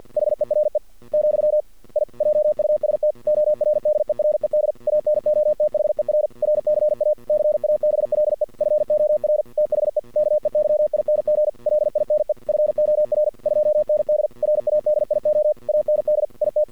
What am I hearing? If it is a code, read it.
Morse code, heard as a keyed tone, about 50 wpm, 610 Hz. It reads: HERE , I JUST ORDERED ANOTHER RCA CABLE FOR THE KEYING LINE FOR 2ND RADIO AND IT
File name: archived / CRSnet